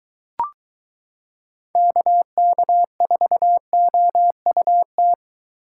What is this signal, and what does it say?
Morse code, heard as a keyed tone, about 23 wpm, 700 Hz. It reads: KK4OUT